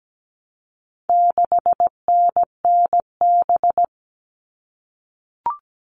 Morse code, heard as a keyed tone, about 17 wpm, 700 Hz. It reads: 6NNB
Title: All Cut Numbers - 4-Digit 17wpm